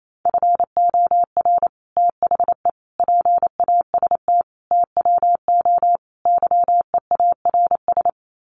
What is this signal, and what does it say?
Morse code, heard as a keyed tone, about 28 wpm, 700 Hz. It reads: FOR THE PAST TWO YEARS
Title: Sets of 5 Words Encore2 28wpm